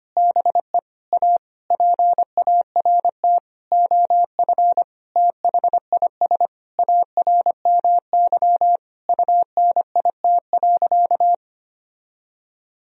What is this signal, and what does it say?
Morse code, heard as a keyed tone, about 25 wpm, 700 Hz. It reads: BE A PART OF THIS ARMY UNIT.
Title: Sentences from Top 1000 Words - Review Part 2 - 25wpm